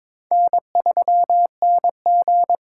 Morse code, heard as a keyed tone, about 22 wpm, 700 Hz. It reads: N3NG
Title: USA Call Signs - 4 Characters - 8x Character Spacing - 22wpm